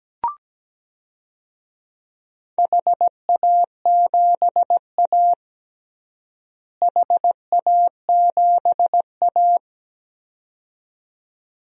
Morse code, heard as a keyed tone, about 17 wpm, 700 Hz. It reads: HA7A HA7A
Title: DXCC-100 Call Signs - 4 Characters - Repeated 2x - 17wpm